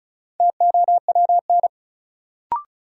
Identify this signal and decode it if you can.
Morse code, heard as a keyed tone, about 34 wpm, 700 Hz. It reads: TOWN
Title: Sentences from Top 500 Words - New Word Drill - 35wpm